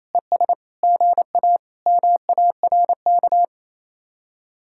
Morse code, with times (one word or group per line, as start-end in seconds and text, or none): 0.1-0.5 ES
0.8-1.6 GA
1.9-3.5 MARK